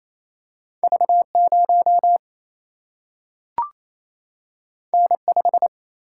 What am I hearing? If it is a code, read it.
Morse code, heard as a keyed tone, about 28 wpm, 700 Hz. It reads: V0 N5
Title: Single-Letter Single-Number 28wpm